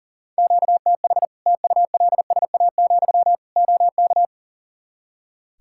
Morse code, heard as a keyed tone, about 40 wpm, 700 Hz. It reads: QTH TULSA, OK
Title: Practice QSO - Exchange 1 - Beginning - 40wpm